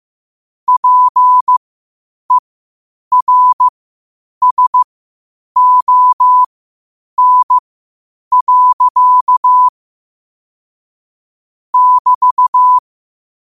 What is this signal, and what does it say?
Morse code, heard as a keyed tone, about 15 wpm, spaced as for 8 wpm, 1000 Hz. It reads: PERSON. <BT>
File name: qotdfiles / 8wpm